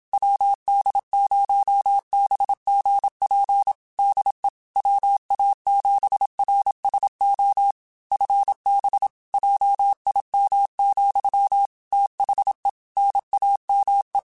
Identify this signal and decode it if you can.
Morse code, heard as a keyed tone, about 27 wpm, 800 Hz. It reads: WD0BGP DE WA7RSO FB JIM, THE NAME
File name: QSOs / QSOs-mp3s / WD0FBT 1@25